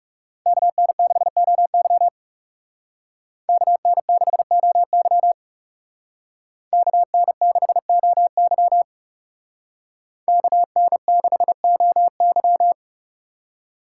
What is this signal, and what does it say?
Morse code, heard as a keyed tone, about 36 wpm, 700 Hz. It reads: KN6OY KN6OY KN6OY KN6OY
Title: Speed Racing - USA 2x2 Call Signs 30wpm